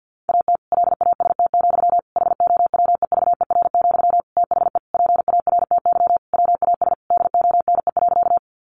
Morse code, heard as a keyed tone, about 50 wpm, 700 Hz. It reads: AT LAST, HOWEVER, THE PARTY WAS DONE.
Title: Sentences from Top 1400 Words 50wpm